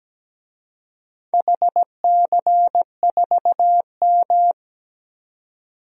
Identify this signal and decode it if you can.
Morse code, heard as a keyed tone, about 17 wpm, 700 Hz. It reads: HC4M
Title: DXCC-100 Call Signs - 4 Characters - 17wpm